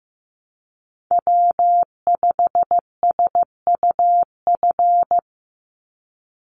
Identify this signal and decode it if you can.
Morse code, heard as a keyed tone, about 15 wpm, 700 Hz. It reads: W5SUF
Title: USA Call Signs 15wpm